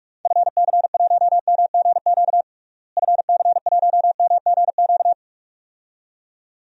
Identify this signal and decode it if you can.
Morse code, heard as a keyed tone, about 42 wpm, 700 Hz. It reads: FC1MGQ FC1MGQ
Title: DXCC-100 Call Signs - 6 Characters - Repeated 2x - 45wpm